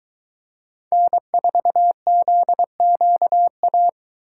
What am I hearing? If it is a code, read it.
Morse code, heard as a keyed tone, about 22 wpm, 700 Hz. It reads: N4ZQA